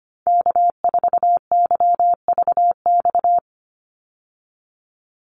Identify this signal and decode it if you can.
Morse code, heard as a keyed tone, about 25 wpm, 700 Hz. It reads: K4YVX
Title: USA Call Signs 25wpm